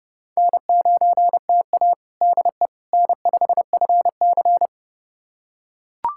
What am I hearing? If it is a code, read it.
Morse code, heard as a keyed tone, about 30 wpm, 700 Hz. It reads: N9TA DE N5FC
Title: USA Callsign DE Callsign - 4 Characters - 30wpm